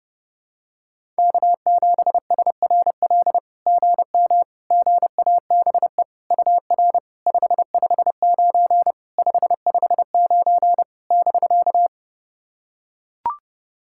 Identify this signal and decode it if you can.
Morse code, heard as a keyed tone, about 28 wpm, 690 Hz. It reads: K7SRL GM GABE UR 559 559 <BK>
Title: SOTA QSO Practice 30wpm